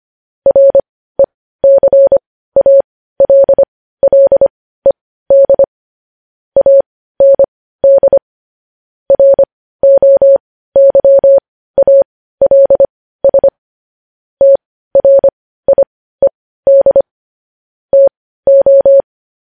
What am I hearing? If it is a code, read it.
Morse code, heard as a keyed tone, about 25 wpm, spaced as for 15 wpm, 550 Hz. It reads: RECALLED AND ROYALS TRIED TO
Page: news-headlines-in-morse-code-at-15-wpm